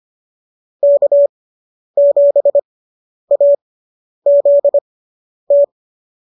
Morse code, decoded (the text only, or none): K7AZT